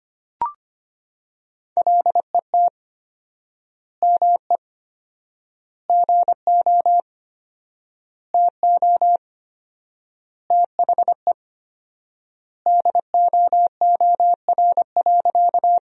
Morse code, heard as a keyed tone, about 25 wpm, 700 Hz. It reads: LET ME GO TO THE DOOR.
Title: 4x Word-Spacing - Sentences from Top 300 Words - 25wpm